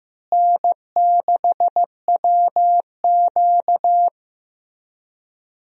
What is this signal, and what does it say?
Morse code, heard as a keyed tone, about 15 wpm, 700 Hz. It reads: N6WQ
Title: USA Call Signs - 4 Characters - Repeated 3x - 15wpm